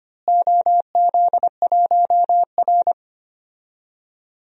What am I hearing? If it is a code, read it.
Morse code, heard as a keyed tone, about 25 wpm, 700 Hz. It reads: OZ1R
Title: DXCC-100 Call Signs - 4 Characters - Repeated 3x - 25wpm